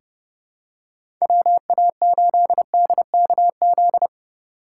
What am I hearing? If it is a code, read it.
Morse code, heard as a keyed tone, about 32 wpm, 700 Hz. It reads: WA8DKZ
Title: USA Call Signs - 6 Characters - 2x Character Spacing - 30wpm